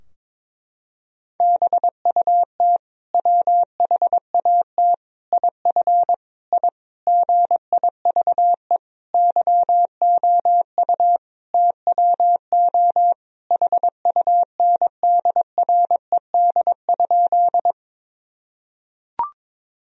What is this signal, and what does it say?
Morse code, heard as a keyed tone, about 22 wpm, 700 Hz. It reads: BUT WHAT IF I GIVE YOU TWO HUNDRED?
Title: Sentences from Top 1000 Words - Review Part 2 - 22wpm